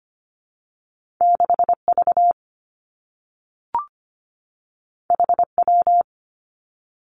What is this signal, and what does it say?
Morse code, heard as a keyed tone, about 25 wpm, 700 Hz. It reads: 6 V H W
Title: Two Challenging Letter-Number Combinations 25wpm